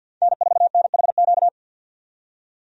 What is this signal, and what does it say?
Morse code, heard as a keyed tone, about 51 wpm, 700 Hz. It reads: N4NHQ
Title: USA Call Signs - 5 Characters - 50wpm